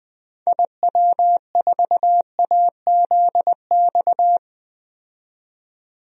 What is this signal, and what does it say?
Morse code, heard as a keyed tone, about 20 wpm, 700 Hz. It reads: IW4AZX
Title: DXCC-100 Call Signs - 6 Characters - 20wpm